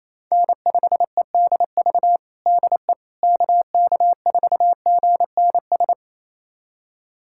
Morse code, heard as a keyed tone, about 28 wpm, 700 Hz. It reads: N5EDV DE KK4GNS
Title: USA Callsign DE Callsign 28wpm